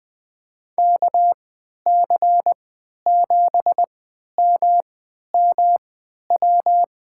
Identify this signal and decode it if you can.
Morse code, heard as a keyed tone, about 20 wpm, spaced as for 11 wpm, 700 Hz. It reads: KC7MMW
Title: USA Call Signs - 6 Characters - 3x Character Spacing - 20wpm